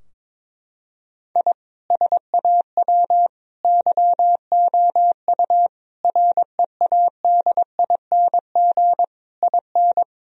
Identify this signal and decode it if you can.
Morse code, heard as a keyed tone, about 22 wpm, 700 Hz. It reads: I SAW YOU READING IN
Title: Sentences from Top 1000 Words - Review Part 4 - 22wpm